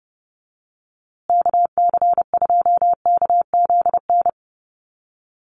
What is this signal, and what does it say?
Morse code, heard as a keyed tone, about 30 wpm, 700 Hz. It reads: KC2KZN
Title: USA Call Signs Repeated 3x 30wpm